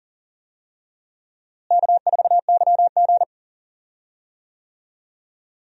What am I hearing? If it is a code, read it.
Morse code, heard as a keyed tone, about 40 wpm, 700 Hz. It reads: K4YG
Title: USA Call Signs - 4 Characters - Repeated 2x - 40wpm